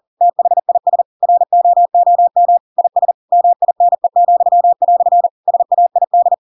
Morse code, heard as a keyed tone, about 40 wpm, 700 Hz. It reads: THIS ROOM IS MINE," SAID
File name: Pinocchio / Pinocchio ch4 40 40 ik1hge